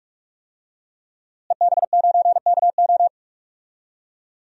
Morse code, heard as a keyed tone, about 42 wpm, 700 Hz. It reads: EB9KO
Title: DXCC-100 Call Signs - 5 Characters - Repeated 3x - 45wpm